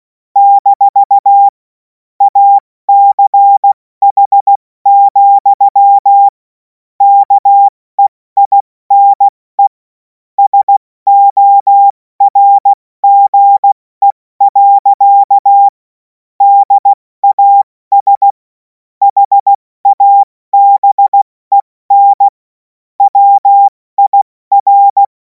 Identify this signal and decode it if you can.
Morse code, heard as a keyed tone, about 16 wpm, 800 Hz. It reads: - ACH, KEINE SORGE. DAS HABEN WIR
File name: Morsekurs / Witze / Witz 28. April 2026 70 F80